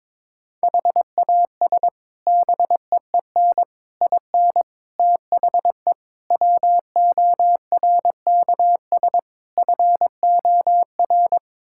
Morse code, heard as a keyed tone, about 22 wpm, 700 Hz. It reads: HAS BEEN IN THE WORKS FOR
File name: practice / 22wpm / Sets of 6 Words - 22wpm